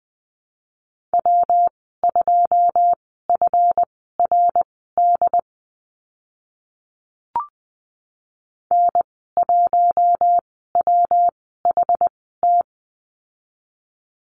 Morse code, heard as a keyed tone, about 20 wpm, spaced as for 14 wpm, 700 Hz. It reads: W2FRD N1WHT
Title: 167 Lesson 27g - Number 7 - Call Signs 20wpm